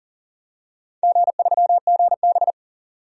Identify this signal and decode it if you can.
Morse code, heard as a keyed tone, about 40 wpm, 700 Hz. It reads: G3GB
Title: DXCC-100 Call Signs - 4 Characters - 40wpm